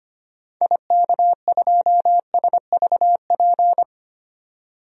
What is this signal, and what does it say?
Morse code, heard as a keyed tone, about 25 wpm, 700 Hz. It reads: IK2SVP
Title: DXCC-100 Call Signs - 6 Characters - 25wpm